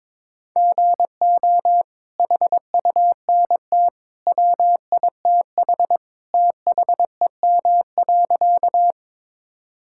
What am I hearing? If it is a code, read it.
Morse code, heard as a keyed tone, about 22 wpm, 700 Hz. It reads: GO HUNT WITH THEM.
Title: Sentences from Top 1600 Words 22wpm